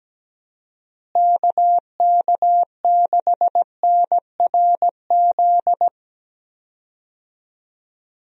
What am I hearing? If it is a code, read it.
Morse code, heard as a keyed tone, about 17 wpm, 700 Hz. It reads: KK6NRZ
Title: USA Call Signs - 6 Characters - Repeated 2x - 17wpm